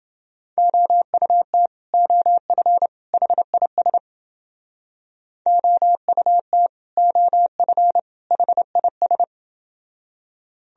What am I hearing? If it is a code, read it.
Morse code, heard as a keyed tone, about 30 wpm, 700 Hz. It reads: OUT OF HIS OUT OF HIS